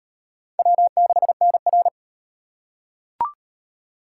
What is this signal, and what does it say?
Morse code, heard as a keyed tone, about 38 wpm, 700 Hz. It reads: W6NR